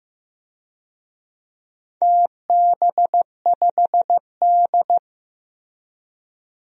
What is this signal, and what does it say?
Morse code, heard as a keyed tone, about 16 wpm, 700 Hz. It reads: TB5D
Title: DXCC-100 Call Signs - 4 Characters - Repeated 3x - 15wpm